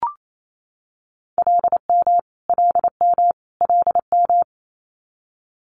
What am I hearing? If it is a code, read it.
Morse code, heard as a keyed tone, about 28 wpm, 700 Hz. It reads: LM LM LM
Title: Multi-letter Phonemes Repeated 3x 28wpm